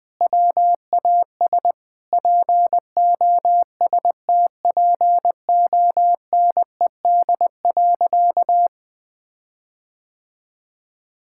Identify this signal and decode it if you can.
Morse code, heard as a keyed tone, about 20 wpm, 700 Hz. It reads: WAS POSTPONED.
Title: Sentences with 8-Word Phrases - 20wpm